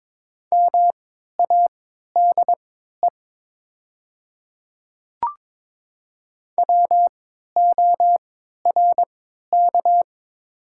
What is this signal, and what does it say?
Morse code, heard as a keyed tone, about 21 wpm, spaced as for 12 wpm, 700 Hz. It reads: MADE WORK
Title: 3x Character Spacing - Top 200 Words - 22wpm